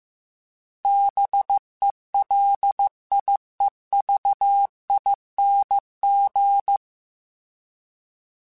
Morse code, heard as a keyed tone, about 15 wpm, 800 Hz. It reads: BELIEVING
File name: Special INGs-mp3s / Special INGs-mp3s / Sngl ing @13